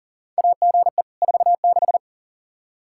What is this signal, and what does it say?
Morse code, heard as a keyed tone, about 40 wpm, 700 Hz. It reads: AGE 46
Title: Practice QSO - Exchange 3 - More Info - 40wpm